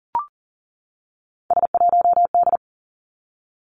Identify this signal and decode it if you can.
Morse code, heard as a keyed tone, about 39 wpm, 700 Hz. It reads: S1D
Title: Three Challenging Letter-Number Combinations 40wpm